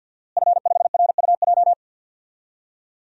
Morse code, heard as a keyed tone, about 53 wpm, 700 Hz. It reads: F5RUJ
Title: DXCC-100 Call Signs - 5 Characters - Repeated 3x - 50wpm